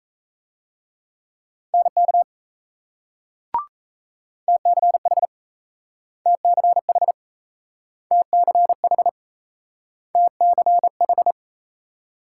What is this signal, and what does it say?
Morse code, heard as a keyed tone, about 38 wpm, 700 Hz. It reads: NK TCH TCH TCH TCH